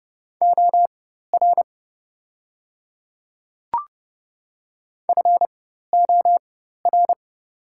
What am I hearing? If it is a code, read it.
Morse code, heard as a keyed tone, about 30 wpm, spaced as for 14 wpm, 700 Hz. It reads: OR FOR
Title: 4x Character Spacing - Top 100 Words - 30wpm